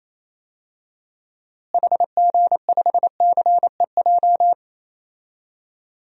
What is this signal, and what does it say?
Morse code, heard as a keyed tone, about 28 wpm, 700 Hz. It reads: HG5CEJ
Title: DXCC-100 Call Signs - 6 Characters - Repeated 3x - 28wpm